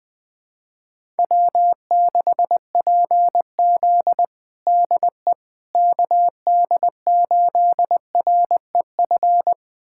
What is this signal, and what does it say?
Morse code, heard as a keyed tone, about 20 wpm, 700 Hz. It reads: W6PZ DE KD8REF
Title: USA Callsign DE Callsign 20wpm